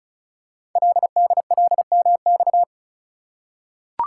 Morse code, heard as a keyed tone, about 36 wpm, 700 Hz. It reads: LDLMX
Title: Code Groups - 5 Letters - 35wpm